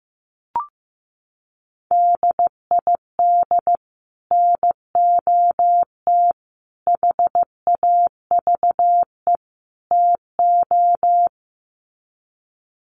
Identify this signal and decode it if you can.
Morse code, heard as a keyed tone, about 15 wpm, 700 Hz. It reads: DID NOT HAVE TO
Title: Sets of 4 Words 15wpm